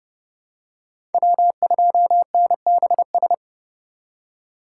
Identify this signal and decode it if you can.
Morse code, heard as a keyed tone, about 30 wpm, 700 Hz. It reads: W2NBS
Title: USA Call Signs - 5 Characters - 4x Character Spacing - 30wpm